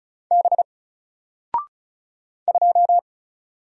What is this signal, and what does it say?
Morse code, heard as a keyed tone, about 36 wpm, 700 Hz. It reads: B 2
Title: Single Letter-Number 35wpm